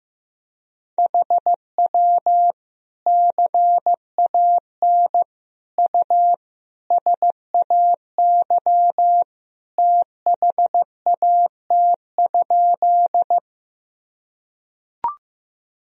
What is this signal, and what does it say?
Morse code, heard as a keyed tone, about 15 wpm, 700 Hz. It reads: HW CAN U SAY THAT?
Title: Sentences from Top 200 Words and Abbreviations 15wpm